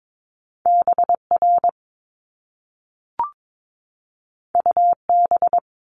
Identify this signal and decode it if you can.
Morse code, heard as a keyed tone, about 22 wpm, 700 Hz. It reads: BR UB
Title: Top Two Letter Combinations in Words 22wpm